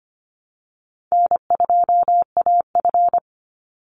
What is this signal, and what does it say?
Morse code, heard as a keyed tone, about 25 wpm, 700 Hz. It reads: N2AF
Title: NA SOTA Chaser Callsigns 25wpm